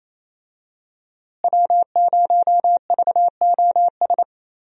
Morse code, heard as a keyed tone, about 28 wpm, 700 Hz. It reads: W0VOS
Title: USA Call Signs - 5 Characters - Repeated 3x - 28wpm